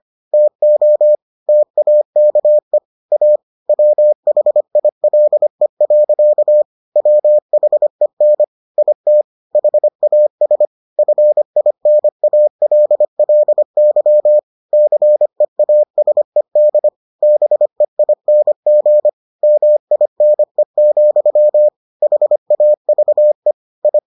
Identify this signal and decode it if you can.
Morse code, heard as a keyed tone, about 25 wpm, 600 Hz. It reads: TO TAKE A WHILE. WHEN IT HAS FINALLY CEASED BEING MINE, HAVE I